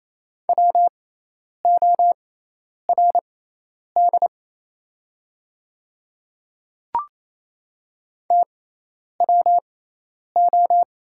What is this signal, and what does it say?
Morse code, heard as a keyed tone, about 28 wpm, spaced as for 10 wpm, 690 Hz. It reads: WORD TWO
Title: 6x Character Spacing - Top 100 Words - 28wpm